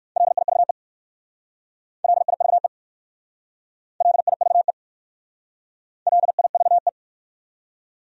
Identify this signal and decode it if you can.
Morse code, heard as a keyed tone, about 54 wpm, 700 Hz. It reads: LIVE LIVE LIVE LIVE
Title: Speed-Racing Top 200 Words 45wpm